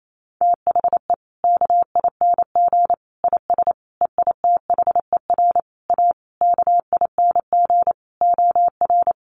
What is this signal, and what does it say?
Morse code, heard as a keyed tone, about 28 wpm, 700 Hz. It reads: THE KING IS EITHER A KING OR
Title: Sentences from Top 600 Words 28wpm